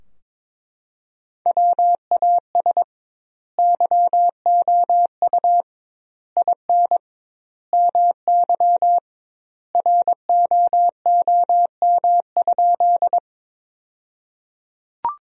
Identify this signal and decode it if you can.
Morse code, heard as a keyed tone, about 22 wpm, 700 Hz. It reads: WAS YOU IN MY ROOM?
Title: 2x Word-Spacing - Sentences from Top 300 Words - 22wpm